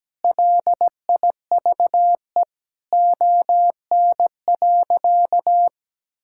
Morse code, heard as a keyed tone, about 17 wpm, 700 Hz. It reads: LIVE ON.
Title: Sentences from Top 1000 Words - Review Part 5 - 17wpm